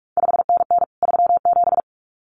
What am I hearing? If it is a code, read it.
Morse code, heard as a keyed tone, about 46 wpm, 700 Hz. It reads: 5NN 37
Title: CQ WW DX Contest Exchange Encore - 45wpm